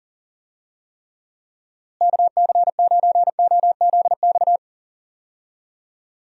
Morse code, heard as a keyed tone, about 40 wpm, 700 Hz. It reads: KC9OZX
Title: USA Call Signs - 6 Characters - Repeated 2x - 40wpm